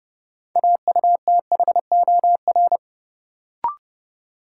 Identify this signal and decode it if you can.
Morse code, heard as a keyed tone, about 30 wpm, 700 Hz. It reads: AUTHOR
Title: Sentences from Top 1200 Words - New Word Drill - 30wpm